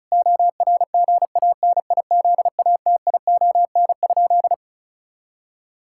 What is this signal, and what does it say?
Morse code, heard as a keyed tone, about 36 wpm, 700 Hz. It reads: ORGANIZATION?
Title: Sentences from Top 1000 Words - Review Part 5 - 35wpm